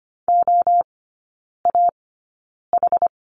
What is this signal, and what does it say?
Morse code, heard as a keyed tone, about 25 wpm, spaced as for 9 wpm, 700 Hz. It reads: OAH
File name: Lesson 17e - Letter u - Three Characters 25wpm